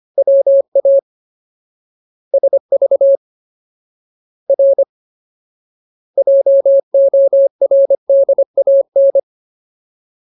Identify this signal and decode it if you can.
Morse code, heard as a keyed tone, about 25 wpm, 550 Hz. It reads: WA SV R JORDAN
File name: Ultimate Word List / 25WPM Ultimate